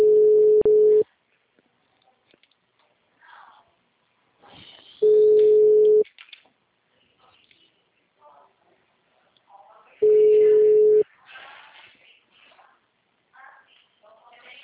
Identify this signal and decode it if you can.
Ringback tone (Europe, 425 Hz), with speech.
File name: out- -2001- - - .35150